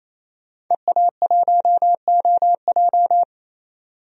Morse code, heard as a keyed tone, about 28 wpm, 700 Hz. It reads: EA1OJ
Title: DXCC-100 Call Signs - 5 Characters - 28wpm